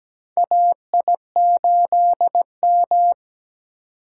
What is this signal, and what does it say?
Morse code, heard as a keyed tone, about 17 wpm, 700 Hz. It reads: AI8M
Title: USA Call Signs - 4 Characters - 4x Character Spacing - 17wpm